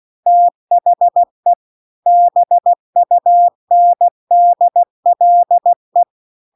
Morse code, heard as a keyed tone, about 16 wpm, 700 Hz. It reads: THE BUNDLE
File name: CW-The-Big-Twoheaded-River-2-16WPM